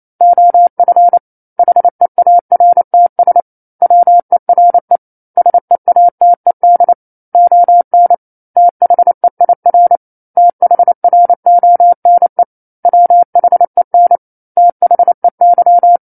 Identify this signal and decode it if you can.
Morse code, heard as a keyed tone, about 28 wpm, 700 Hz. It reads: OF HEARTS WERE SEATED ON THEIR THRONE WHEN THEY
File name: 09-Chapter-09 28 WPM